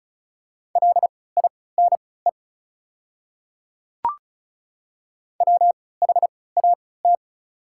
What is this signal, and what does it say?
Morse code, heard as a keyed tone, about 36 wpm, spaced as for 20 wpm, 700 Hz. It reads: LINE WHAT
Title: 3x Character Spacing - Top 200 Words - 35wpm